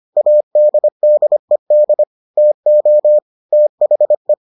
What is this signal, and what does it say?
Morse code, heard as a keyed tone, about 25 wpm, 600 Hz. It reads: ADDED TO THE